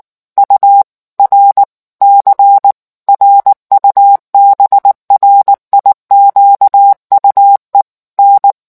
Morse code, heard as a keyed tone, about 19 wpm, 800 Hz. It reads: URC RUBRIQUE N